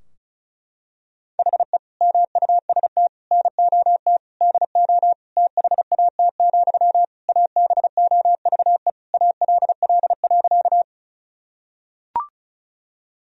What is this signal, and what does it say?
Morse code, heard as a keyed tone, about 34 wpm, 700 Hz. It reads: HE MUST NOT DO THAT, ABOVE ALL.
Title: Sentences from Top 1000 Words - Review Part 2 - 35wpm